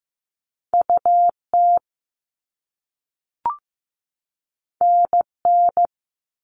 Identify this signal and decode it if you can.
Morse code, heard as a keyed tone, about 15 wpm, 700 Hz. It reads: UT NN
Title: All Cut Numbers - 2-Digit 15wpm